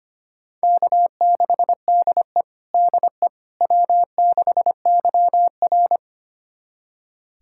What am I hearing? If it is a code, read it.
Morse code, heard as a keyed tone, about 25 wpm, 700 Hz. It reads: K6DE DE W6YR
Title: USA Callsign DE Callsign - 4 Characters - 25wpm